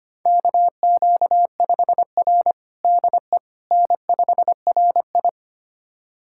Morse code, heard as a keyed tone, about 25 wpm, 700 Hz. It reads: KQ5R DE N5RI
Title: USA Callsign DE Callsign - 4 Characters - 25wpm